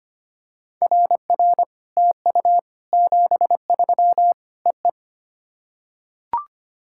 Morse code, heard as a keyed tone, about 25 wpm, 700 Hz. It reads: RR TU 73 EE
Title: SOTA QSO Practice 25wpm